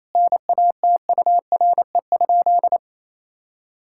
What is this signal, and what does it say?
Morse code, heard as a keyed tone, about 28 wpm, 700 Hz. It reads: NATURE?
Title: Sentences from Top 1000 Words - Review Part 6 - 28wpm